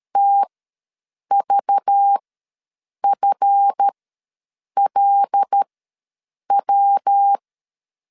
Morse code, heard as a keyed tone, about 13 wpm, spaced as for 7 wpm, 800 Hz. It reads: TVFLW
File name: LEZIONI / 7 GF9Z / 7 GF ws 12wpm 3140